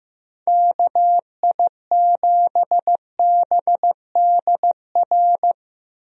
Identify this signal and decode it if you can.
Morse code, heard as a keyed tone, about 15 wpm, 700 Hz. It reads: KI7BDR
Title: USA Call Signs - 6 Characters - Repeated 3x - 15wpm